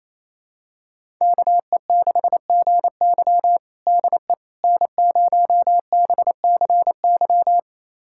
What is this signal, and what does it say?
Morse code, heard as a keyed tone, about 28 wpm, 700 Hz. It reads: KE6GY DE N0BCY
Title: USA Callsign DE Callsign 28wpm